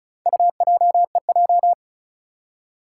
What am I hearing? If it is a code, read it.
Morse code, heard as a keyed tone, about 36 wpm, 700 Hz. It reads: UJEJ
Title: Four Challenging Letter-Number Combinations 35wpm